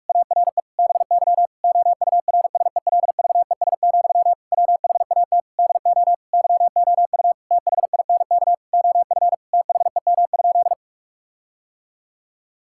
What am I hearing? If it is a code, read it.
Morse code, heard as a keyed tone, about 46 wpm, 700 Hz. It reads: ARE BY OURSELVES, WHAT DO YOU THINK OF THEM?
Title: Sentences from Top 1600 Words 45wpm